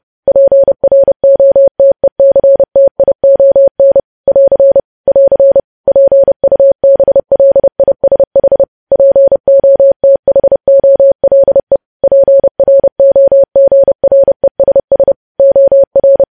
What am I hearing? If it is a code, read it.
Morse code, heard as a keyed tone, about 30 wpm, 550 Hz. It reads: PROTECTION <AR> <AR> PUBLISH POTHOLE PROGRESS OR
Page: news headlines in morse code at 30 wpm